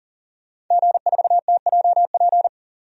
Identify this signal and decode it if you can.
Morse code, heard as a keyed tone, about 40 wpm, 700 Hz. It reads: G4TJP
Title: DXCC-100 Call Signs - 5 Characters - 40wpm